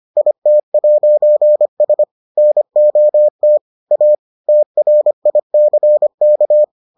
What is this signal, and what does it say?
Morse code, heard as a keyed tone, about 25 wpm, 600 Hz. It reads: IT'S NOT A TRICK